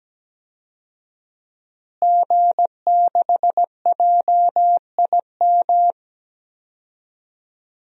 Morse code, heard as a keyed tone, about 17 wpm, 700 Hz. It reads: G6JIM
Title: DXCC-100 Call Signs - 5 Characters - Repeated 2x - 17wpm